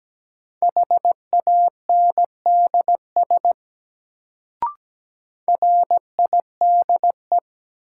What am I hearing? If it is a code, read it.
Morse code, heard as a keyed tone, about 17 wpm, 700 Hz. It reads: HANDS RIDE
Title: Sentences from Top 1000 Words - Review Part 6 - New Word Drill - 17wpm